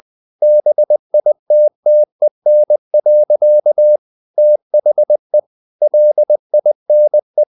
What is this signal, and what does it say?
Morse code, heard as a keyed tone, about 20 wpm, 600 Hz. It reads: BITTEN. THE LINE
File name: the old man and the sea part 2 20wpm